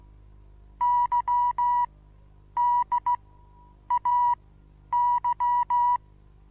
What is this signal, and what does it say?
Morse code, heard as a keyed tone, about 15 wpm, spaced as for 9 wpm, 970 Hz. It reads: YDAY
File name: Onlinelehrgang / morsen / DJ4UF Morsen25